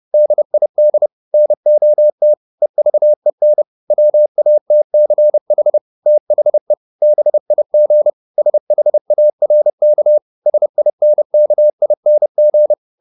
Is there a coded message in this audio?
Yes: Morse code from 0.1 to 12.7 s, heard as a keyed tone, about 30 wpm, 600 Hz. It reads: DID NOT EVEN WATCH THE BIG SHARK SINKING